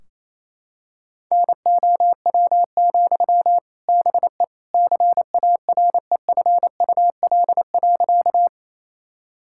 Morse code, heard as a keyed tone, about 28 wpm, 700 Hz. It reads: NOW, BE CAREFUL.
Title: Sentences from Top 1600 Words 28wpm